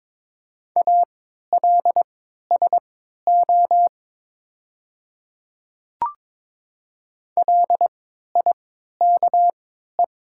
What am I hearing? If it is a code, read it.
Morse code, heard as a keyed tone, about 23 wpm, spaced as for 13 wpm, 700 Hz. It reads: ALSO LIKE
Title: 3x Character Spacing - Top 300 Words - 22wpm